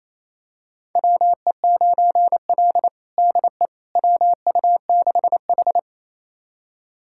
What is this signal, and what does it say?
Morse code, heard as a keyed tone, about 28 wpm, 700 Hz. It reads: WE9L DE WU6H
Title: USA Callsign DE Callsign - 4 Characters - 28wpm